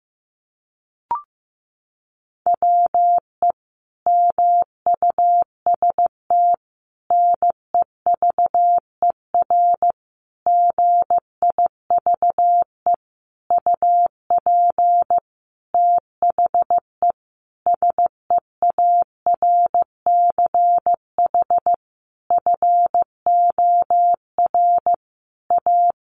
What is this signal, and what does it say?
Morse code, heard as a keyed tone, about 15 wpm, 700 Hz. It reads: WE MUST NEVER GIVE UP THE SEARCH FOR A
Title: Sentences from Top 1500 Words 15wpm